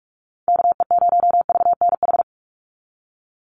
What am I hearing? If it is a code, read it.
Morse code, heard as a keyed tone, about 46 wpm, 690 Hz. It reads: KE0VNH
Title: USA Call Signs 45wpm